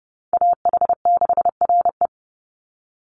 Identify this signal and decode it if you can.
Morse code, heard as a keyed tone, about 30 wpm, 700 Hz. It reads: AH6RE
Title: Call Signs Repeated 3x - 30wpm